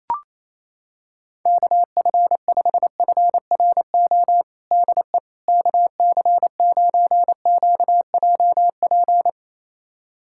Morse code, heard as a keyed tone, about 28 wpm, 700 Hz. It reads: KF5FRO DE KC9QJP
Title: USA Callsign DE Callsign - 6 Characters - 28wpm